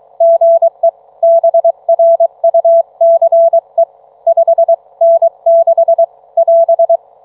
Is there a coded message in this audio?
Yes: Morse code from 0.2 to 7.0 s, heard as a keyed tone, about 23 wpm, 660 Hz. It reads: GE BRUCE 5N6 <AS>